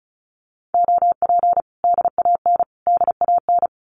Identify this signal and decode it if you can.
Morse code, heard as a keyed tone, about 34 wpm, 700 Hz. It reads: OP DAN DAN
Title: Common QSO Phrases Encore 35wpm